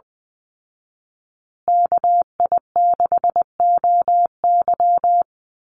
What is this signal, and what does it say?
Morse code, heard as a keyed tone, about 20 wpm, 700 Hz. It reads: KI6OY
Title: Call Signs Repeated 2x - 20wpm